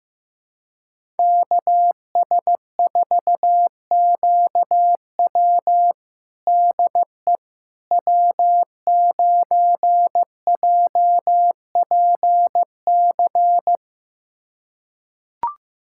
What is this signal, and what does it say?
Morse code, heard as a keyed tone, about 15 wpm, 700 Hz. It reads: KS4QW DE W9JPC
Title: USA Callsign DE Callsign - 5 Characters - 15wpm